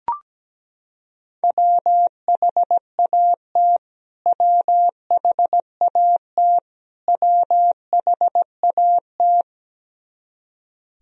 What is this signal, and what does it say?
Morse code, heard as a keyed tone, about 17 wpm, 700 Hz. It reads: WHAT WHAT WHAT
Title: Top 100 Words Repeated 17wpm